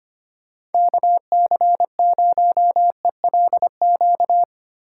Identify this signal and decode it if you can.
Morse code, heard as a keyed tone, about 30 wpm, 700 Hz. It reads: KC0ELQ
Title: Speed Racing - USA 2x3 Call Signs 25wpm